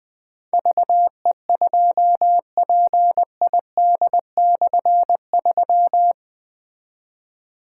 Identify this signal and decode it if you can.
Morse code, heard as a keyed tone, about 20 wpm, 700 Hz. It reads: VE2PID/3
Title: K1USN SST QSO Practice - 20wpm